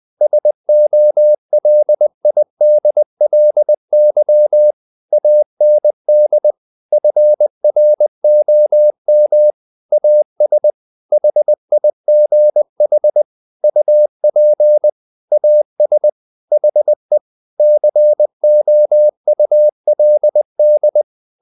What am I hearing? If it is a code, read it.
Morse code, heard as a keyed tone, about 20 wpm, 600 Hz. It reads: SOLIDLY AND FROM AS HIGH UP AS HE COULD
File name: the old man and the sea part 2 20wpm